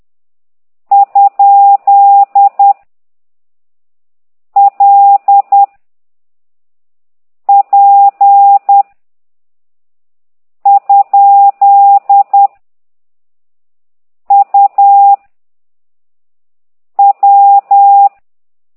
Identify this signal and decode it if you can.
Morse code, heard as a keyed tone, about 10 wpm, 800 Hz. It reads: ? L P ? U W